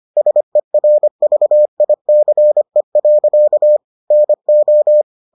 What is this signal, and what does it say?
Morse code, heard as a keyed tone, about 25 wpm, 600 Hz. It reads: SERVICE. NO